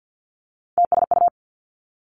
Morse code, heard as a keyed tone, about 50 wpm, 700 Hz. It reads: TSU
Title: Three Challenging Letter-Number Combinations 50wpm